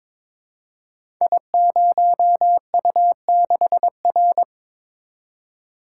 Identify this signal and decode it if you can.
Morse code, heard as a keyed tone, about 23 wpm, 700 Hz. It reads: I0U6R
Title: Code Groups - 5 Letter:Numbers - 22wpm